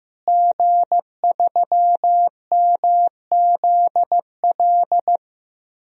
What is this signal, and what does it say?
Morse code, heard as a keyed tone, about 15 wpm, 700 Hz. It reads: G3MZL
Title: DXCC-100 Call Signs - 5 Characters - Repeated 3x - 15wpm